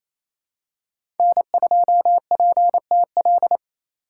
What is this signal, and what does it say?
Morse code, heard as a keyed tone, about 28 wpm, 700 Hz. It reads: N2PTL
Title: USA Call Signs - 5 Characters - 2x Character Spacing - 28wpm